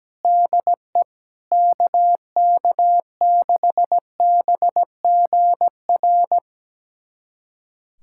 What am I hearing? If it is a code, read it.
Morse code, heard as a keyed tone, about 17 wpm, 700 Hz. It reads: DE KK6BGR
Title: USA Callsign DE Callsign - 6 Characters - 17wpm